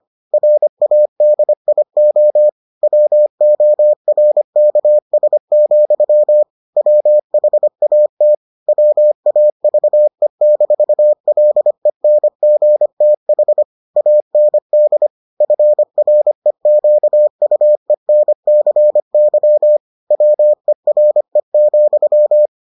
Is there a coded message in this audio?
Yes: Morse code from 0.3 to 22.5 s, heard as a keyed tone, about 25 wpm, 600 Hz. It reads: RADIO WORKS, WHAT WAVE-LENGTH AND FREQUENCY WERE,